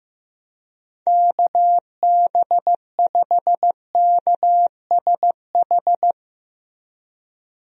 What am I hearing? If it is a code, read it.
Morse code, heard as a keyed tone, about 15 wpm, 700 Hz. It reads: KB5KSH
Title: USA Call Signs - 6 Characters - Repeated 2x - 15wpm